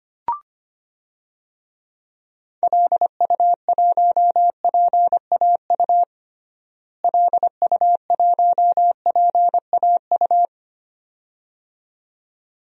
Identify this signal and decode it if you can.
Morse code, heard as a keyed tone, about 25 wpm, 700 Hz. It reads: LU1PAU LU1PAU
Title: DXCC-100 Call Signs - 6 Characters - Repeated 2x - 25wpm